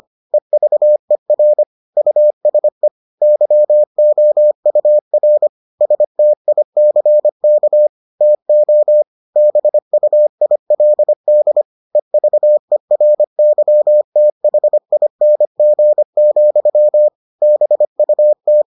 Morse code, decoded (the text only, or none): EVER USE YOUR STICK TO BUILD EVERYTHING, BUT